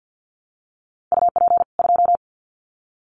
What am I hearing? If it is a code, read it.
Morse code, heard as a keyed tone, about 50 wpm, 700 Hz. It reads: UP 2
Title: Common QSO Phrases 50wpm